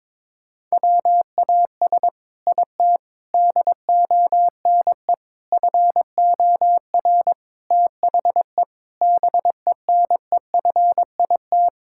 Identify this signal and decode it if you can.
Morse code, heard as a keyed tone, about 22 wpm, 700 Hz. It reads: WAS IT DONE FOR THE BENEFIT
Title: Sentences from Top 1000 Words - Review Part 6 - 22wpm